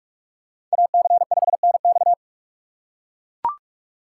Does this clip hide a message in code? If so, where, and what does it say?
Yes: Morse code from 0.7 to 2.1 s, heard as a keyed tone, about 40 wpm, 700 Hz. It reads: AC5NX